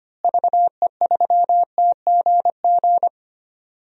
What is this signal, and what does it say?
Morse code, heard as a keyed tone, about 25 wpm, 700 Hz. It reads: VE3TGG
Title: DXCC-100 Call Signs - 6 Characters - Repeated 3x - 25wpm